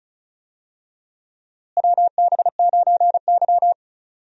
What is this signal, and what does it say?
Morse code, heard as a keyed tone, about 34 wpm, 700 Hz. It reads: WB9Y
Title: USA Call Signs - 4 Characters - Repeated 3x - 35wpm